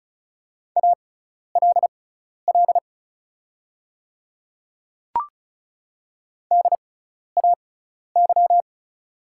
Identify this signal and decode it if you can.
Morse code, heard as a keyed tone, about 36 wpm, spaced as for 12 wpm, 700 Hz. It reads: ALL DAY
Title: 6x Character Spacing - Top 100 Words - 35wpm